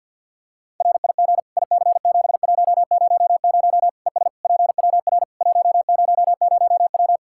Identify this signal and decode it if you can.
Morse code, heard as a keyed tone, about 50 wpm, 700 Hz. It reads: RIG IC7100 ES PWR 100W
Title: K1USN SST QSO Practice 50wpm